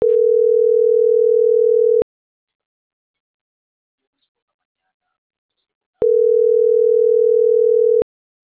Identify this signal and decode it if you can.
Ringback tone (North America), 440 + 480 Hz.